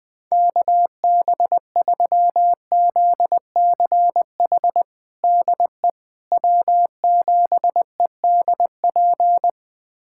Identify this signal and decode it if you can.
Morse code, heard as a keyed tone, about 20 wpm, 700 Hz. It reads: KB3ZCH DE W7EDP
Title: USA Callsign DE Callsign 20wpm